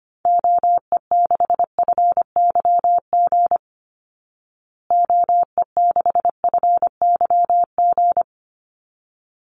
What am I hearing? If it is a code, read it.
Morse code, heard as a keyed tone, about 25 wpm, 700 Hz. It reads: OE6FYG OE6FYG
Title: Call Signs Repeated 3x - 25wpm